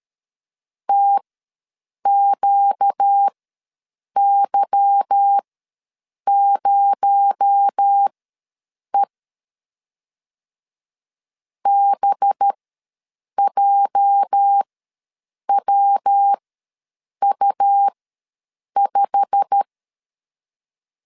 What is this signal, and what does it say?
Morse code, heard as a keyed tone, about 13 wpm, spaced as for 7 wpm, 800 Hz. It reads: TQY0E BJWU5
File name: LEZIONI / 8 YJ45 / 8 YJ45 ws 12wpm 3146